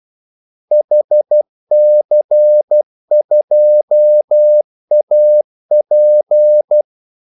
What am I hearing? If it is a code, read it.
Morse code, heard as a keyed tone, about 12 wpm, 600 Hz. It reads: HC2AP
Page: 2 ekstra ordmellemrum mellem kaldesignalerne